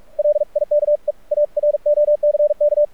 Morse code, heard as a keyed tone, about 43 wpm, 590 Hz. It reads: LIKE A ROCK
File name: archived / CRSnet